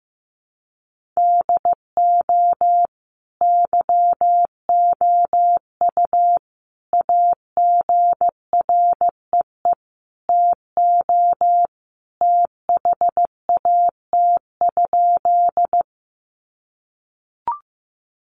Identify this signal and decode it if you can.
Morse code, heard as a keyed tone, about 15 wpm, 700 Hz. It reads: DO YOU AGREE TO THAT?
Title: Sentences from Top 1100 Words 15wpm